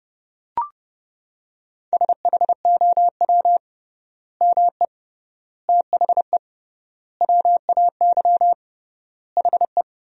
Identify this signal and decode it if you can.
Morse code, heard as a keyed tone, about 30 wpm, 700 Hz. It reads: SHOW ME THE WAY HE
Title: 3x Word-Spacing - Sentences from Top 200 Words - 30wpm